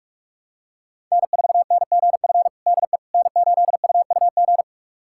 Morse code, heard as a keyed tone, about 46 wpm, 700 Hz. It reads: N4NGF DE N8UUG
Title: USA Callsign DE Callsign - 5 Characters - 45wpm